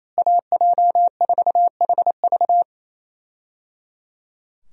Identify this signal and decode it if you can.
Morse code, heard as a keyed tone, about 28 wpm, 700 Hz. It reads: AJ4HV
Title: USA Call Signs - 5 Characters - Repeated 3x - 28wpm